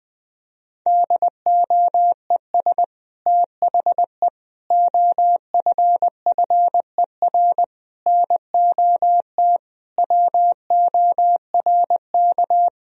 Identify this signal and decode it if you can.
Morse code, heard as a keyed tone, about 20 wpm, 700 Hz. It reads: DOES THE OFFER NOT WORK
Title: Sentences from Top 1000 Words - Review Part 4 - 20wpm